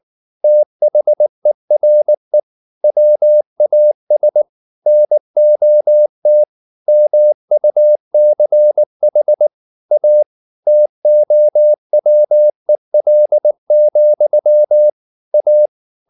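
Morse code, heard as a keyed tone, about 19 wpm, 600 Hz. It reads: THERE WAS NOT MUCH A TOWEL, A
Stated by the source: Morse code at 19 wpm from CW med Gnister, reading Ken Follett's novel Never in English.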